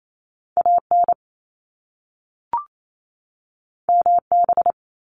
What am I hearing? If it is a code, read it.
Morse code, heard as a keyed tone, about 28 wpm, 700 Hz. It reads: AN MB
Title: Top Two Letter Combinations in Words 28wpm